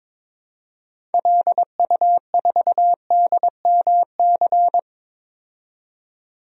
Morse code, heard as a keyed tone, about 21 wpm, 700 Hz. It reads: LU4DMC
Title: DXCC-100 Call Signs - 6 Characters - 22wpm